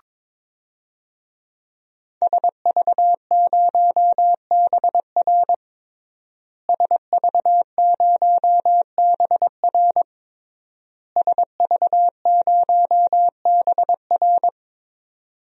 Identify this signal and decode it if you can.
Morse code, heard as a keyed tone, about 22 wpm, 700 Hz. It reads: SV0BR SV0BR SV0BR
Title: DXCC-100 Call Signs - 5 Characters - Repeated 3x - 22wpm